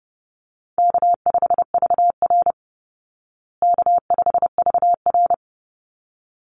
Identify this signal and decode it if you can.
Morse code, heard as a keyed tone, about 30 wpm, 700 Hz. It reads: K5VR K5VR
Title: Call Signs Repeated 3x - 30wpm